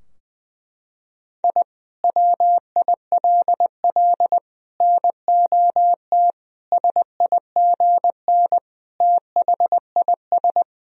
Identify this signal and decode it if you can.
Morse code, heard as a keyed tone, about 20 wpm, 700 Hz. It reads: I WILL NOT SIGN THIS
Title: Sentences from Top 1000 Words - Review Part 2 - 20wpm